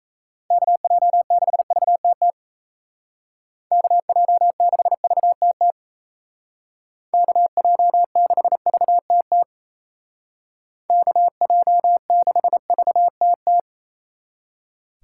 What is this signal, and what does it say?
Morse code, heard as a keyed tone, about 34 wpm, 700 Hz. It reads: KJ6VTT KJ6VTT KJ6VTT KJ6VTT
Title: Speed Racing - USA 2x3 Call Signs 28wpm